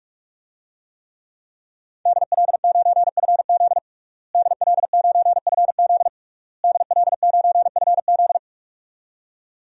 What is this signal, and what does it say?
Morse code, heard as a keyed tone, about 42 wpm, 700 Hz. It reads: DL9FZ DL9FZ DL9FZ
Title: DXCC-100 Call Signs - 5 Characters - Repeated 3x - 45wpm